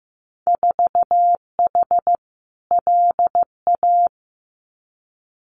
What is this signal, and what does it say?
Morse code, heard as a keyed tone, about 15 wpm, 700 Hz. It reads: 4H LA
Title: Winter Field Day Exchanges Encore 15wpm